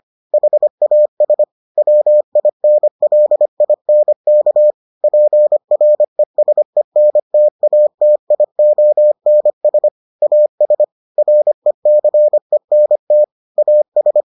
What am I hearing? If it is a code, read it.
Morse code, heard as a keyed tone, about 25 wpm, 600 Hz. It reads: HAS WINLINK PRESENTATIONS AS RECENT AS